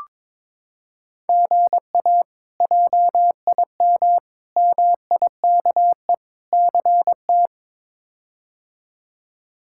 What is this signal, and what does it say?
Morse code, heard as a keyed tone, about 22 wpm, 700 Hz. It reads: GA JIM MIKE CT
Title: K1USN SST QSO Practice - 22wpm